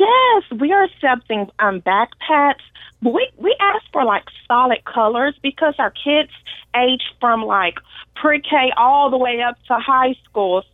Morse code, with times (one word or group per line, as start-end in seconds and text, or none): none